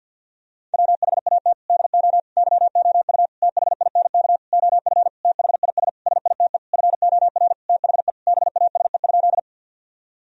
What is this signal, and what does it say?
Morse code, heard as a keyed tone, about 48 wpm, 700 Hz. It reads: WHAT DO YOU THINK OF THIS SITE FOR THE BASE?
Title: Sentences from Top 1000 Words - Review Part 1 - 50wpm